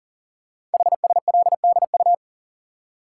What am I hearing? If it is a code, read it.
Morse code, heard as a keyed tone, about 42 wpm, 700 Hz. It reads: HSLDU
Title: Code Groups - 5 Letter:Numbers - 40wpm